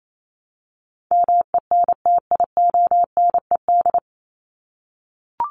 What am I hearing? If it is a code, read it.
Morse code, heard as a keyed tone, about 28 wpm, 700 Hz. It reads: MENTIONED
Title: ED Suffix - Top 100 Words with Base Words 28wpm